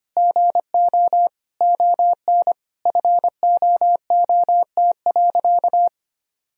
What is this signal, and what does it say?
Morse code, heard as a keyed tone, about 25 wpm, 700 Hz. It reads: GO ON FOOT.
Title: Sentences from Top 1000 Words - Review Part 3 - 25wpm